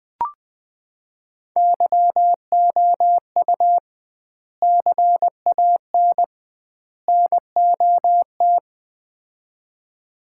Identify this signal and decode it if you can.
Morse code, heard as a keyed tone, about 20 wpm, 700 Hz. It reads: YOU CAN NOT
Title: 2x Word-Spacing - Sets of 3 Words - 20wpm